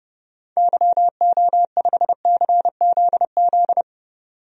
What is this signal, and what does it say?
Morse code, heard as a keyed tone, about 31 wpm, 700 Hz. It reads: YO5CZZ
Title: DXCC-100 Call Signs - 6 Characters - 30wpm